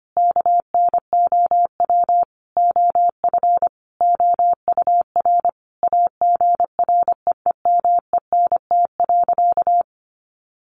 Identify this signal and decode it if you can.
Morse code, heard as a keyed tone, about 25 wpm, 700 Hz. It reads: KNOW OF OUR AGREEMENT.
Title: Sentences from Top 1500 Words 25wpm